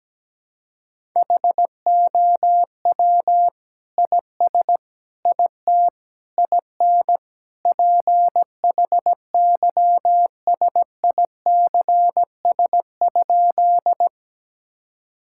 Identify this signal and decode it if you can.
Morse code, heard as a keyed tone, about 17 wpm, 700 Hz. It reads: HOW IS IT IN PHYSICS?
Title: Sentences from Top 1000 Words - Review Part 5 - 17wpm